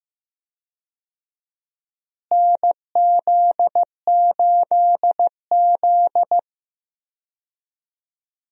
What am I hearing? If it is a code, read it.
Morse code, heard as a keyed tone, about 15 wpm, 700 Hz. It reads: NZ8Z
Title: USA Call Signs - 4 Characters - Repeated 2x - 15wpm